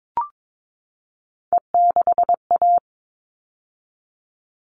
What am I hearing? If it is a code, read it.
Morse code, heard as a keyed tone, about 22 wpm, 700 Hz. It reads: E6A
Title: All Cut Numbers - 3-Digit 22wpm